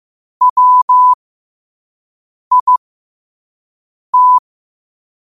Morse code, heard as a keyed tone, about 15 wpm, spaced as for 5 wpm, 1000 Hz. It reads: WIT